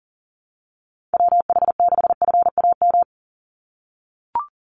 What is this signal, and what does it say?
Morse code, heard as a keyed tone, about 40 wpm, 700 Hz. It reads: WH6FAM
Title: USA Call Signs 40wpm